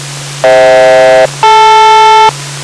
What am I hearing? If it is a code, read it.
Aircraft SELCAL, heard as two pairs of tones. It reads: FH-DL